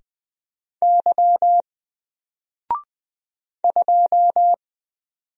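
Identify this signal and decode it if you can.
Morse code, heard as a keyed tone, about 20 wpm, 700 Hz. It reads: Y 2
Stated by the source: Single letter and number Morse drill at 20 wpm, from N4LCD.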